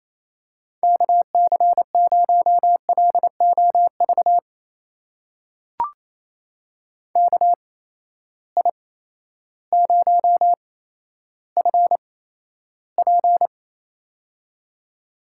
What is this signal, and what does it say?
Morse code, heard as a keyed tone, about 28 wpm, spaced as for 13 wpm, 700 Hz. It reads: KC0LOV KI0FP
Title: USA Call Signs - 8x Character Spacing - 28wpm